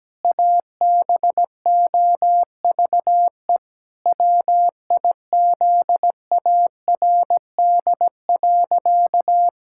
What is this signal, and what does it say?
Morse code, heard as a keyed tone, about 17 wpm, 700 Hz. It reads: ABOVE WIZARD.
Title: Pangram Challenge v2 - 17wpm